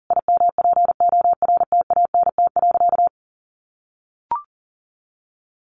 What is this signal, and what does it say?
Morse code, heard as a keyed tone, about 40 wpm, 700 Hz. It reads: IMPORTANT.
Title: A Sound Of Thunder 40wpm